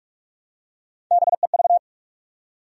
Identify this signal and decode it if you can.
Morse code, heard as a keyed tone, about 46 wpm, 700 Hz. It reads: BEV
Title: USA Common Names - Max 3 Letters - 45wpm